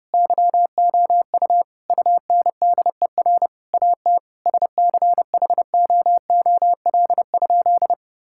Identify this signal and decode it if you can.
Morse code, heard as a keyed tone, about 30 wpm, 700 Hz. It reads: YOU UNDER AT SCHOOL?
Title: Sentences from Top 1000 Words - Review Part 6 - 30wpm